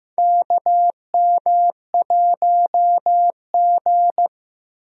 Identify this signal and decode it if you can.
Morse code, heard as a keyed tone, about 15 wpm, 700 Hz. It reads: KM1G
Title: USA Call Signs - 4 Characters - 15wpm